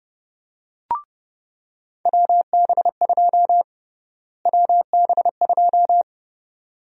Morse code, heard as a keyed tone, about 30 wpm, 700 Hz. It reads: WB2 WB2
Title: USA Call Sign Prefixes - Repeated 2x - 30wpm